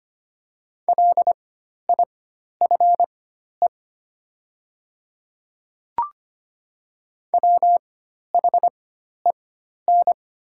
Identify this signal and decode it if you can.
Morse code, heard as a keyed tone, about 25 wpm, spaced as for 12 wpm, 700 Hz. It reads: LIFE WHEN
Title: 4x Character Spacing - Top 500 Words - 25wpm